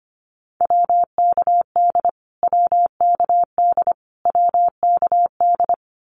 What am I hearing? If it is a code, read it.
Morse code, heard as a keyed tone, about 25 wpm, 700 Hz. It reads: WKD WKD WKD
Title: Abbreviations Repeated 3x - 25wpm